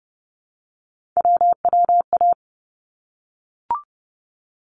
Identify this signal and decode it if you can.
Morse code, heard as a keyed tone, about 30 wpm, 700 Hz. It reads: WWA
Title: ARRL Sections 30wpm